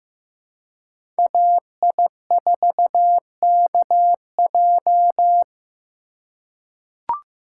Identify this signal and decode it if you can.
Morse code, heard as a keyed tone, about 15 wpm, 700 Hz. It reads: AI4KJ
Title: USA Call Signs - 5 Characters - 2x Character Spacing - 15wpm